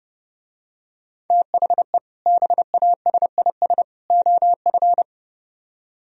Morse code, heard as a keyed tone, about 29 wpm, 700 Hz. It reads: THE BASIS OF